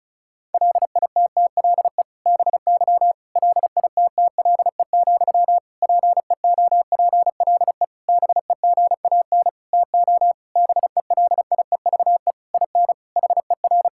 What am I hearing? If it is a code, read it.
Morse code, heard as a keyed tone, about 36 wpm, 700 Hz. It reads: LITTLE BY LITTLE, PEOPLE BEGAN TO BELIEVE IN HER
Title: Sentences from Top 1600 Words 35wpm